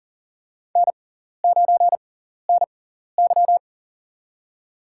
Morse code, heard as a keyed tone, about 40 wpm, spaced as for 14 wpm, 700 Hz. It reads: N9NY
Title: USA Call Signs - 4 Characters - 6x Character Spacing - 40wpm